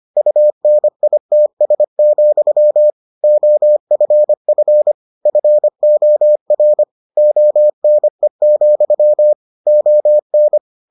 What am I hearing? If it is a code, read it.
Morse code, heard as a keyed tone, about 25 wpm, 600 Hz. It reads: UNITS, OFF FOR ONE, ON